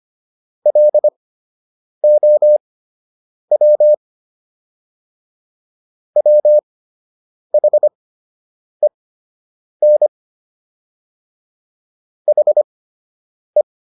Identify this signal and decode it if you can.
Morse code, heard as a keyed tone, about 25 wpm, spaced as for 8 wpm, 600 Hz. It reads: LOW WHEN HE